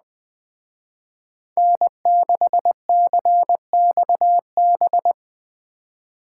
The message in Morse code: N6CXB

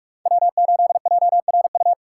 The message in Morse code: W8JRU

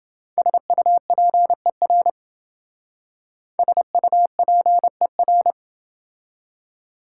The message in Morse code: SUPER SUPER

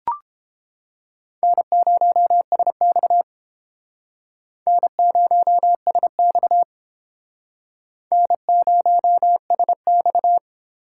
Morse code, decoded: N0SX N0SX N0SX

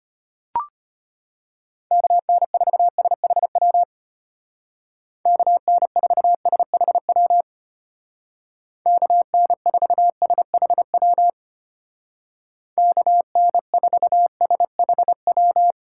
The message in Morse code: KN4SHW KN4SHW KN4SHW KN4SHW

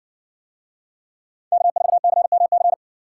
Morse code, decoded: K4CMC

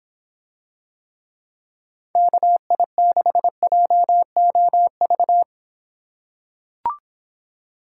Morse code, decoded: KI6JOV

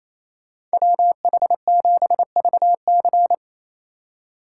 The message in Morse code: WH7VC

